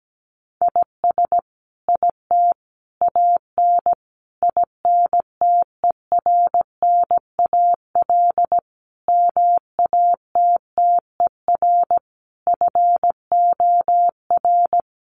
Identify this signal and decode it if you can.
Morse code, heard as a keyed tone, about 17 wpm, 700 Hz. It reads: IS IT AN INTERNAL MATTER FOR
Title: Sentences from Top 1400 Words 17wpm